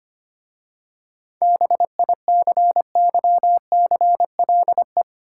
Morse code, heard as a keyed tone, about 25 wpm, 700 Hz. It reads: BICYCLE
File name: practice / 25wpm / Easy Words with Hints - 25wpm